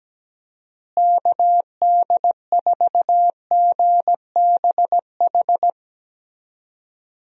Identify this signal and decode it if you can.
Morse code, heard as a keyed tone, about 17 wpm, 700 Hz. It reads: KD4GBH
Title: USA Call Signs - 6 Characters - Repeated 2x - 17wpm